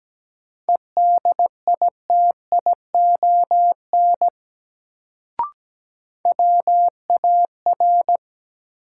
Morse code, heard as a keyed tone, about 17 wpm, 700 Hz. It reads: EDITION WAR
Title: Sets of 5 Words - New Word Drill - 17wpm